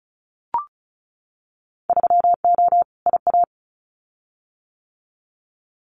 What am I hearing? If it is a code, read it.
Morse code, heard as a keyed tone, about 34 wpm, 700 Hz. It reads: 3O IA
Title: Winter Field Day Exchanges 35wpm